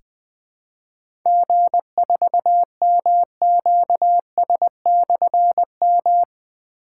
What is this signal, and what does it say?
Morse code, heard as a keyed tone, about 20 wpm, 700 Hz. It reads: G4MQS/M
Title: Difficult Call Signs 20wpm